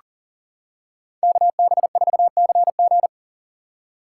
Morse code, heard as a keyed tone, about 40 wpm, 700 Hz. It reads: KB4CG
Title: K1USN SST QSO Practice - 40wpm